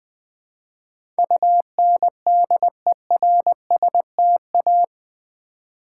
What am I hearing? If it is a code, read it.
Morse code, heard as a keyed tone, about 20 wpm, 700 Hz. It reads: UNDERSTA
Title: Build A Word - 4 Letters At A Time - 10-Letter Words - 20wpm